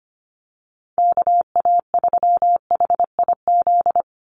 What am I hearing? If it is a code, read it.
Morse code, heard as a keyed tone, about 25 wpm, 700 Hz. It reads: KA3HIZ
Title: USA Call Signs 25wpm